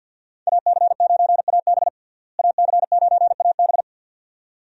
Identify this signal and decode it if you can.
Morse code, heard as a keyed tone, about 52 wpm, 700 Hz. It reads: AC9AB AC9AB
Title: USA Call Signs - 5 Characters - Repeated 2x - 50wpm